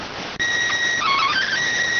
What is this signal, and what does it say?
SSTV (slow-scan television), PD 290.